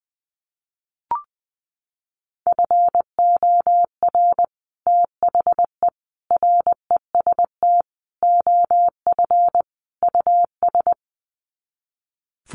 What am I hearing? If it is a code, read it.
Morse code, heard as a keyed tone, about 20 wpm, 700 Hz. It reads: FOR THE REST OF US
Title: Sets of 5 Words Encore 20wpm